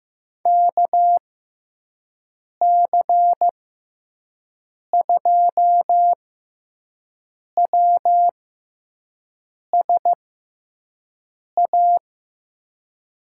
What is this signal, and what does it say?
Morse code, heard as a keyed tone, about 15 wpm, spaced as for 5 wpm, 700 Hz. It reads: KC2WSA
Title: USA Call Signs - 6 Characters - 6x Character Spacing - 15wpm